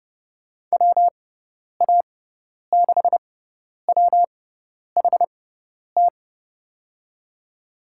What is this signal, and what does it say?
Morse code, heard as a keyed tone, about 32 wpm, spaced as for 10 wpm, 700 Hz. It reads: WA6WHT
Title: USA Call Signs - 6 Characters - 6x Character Spacing - 30wpm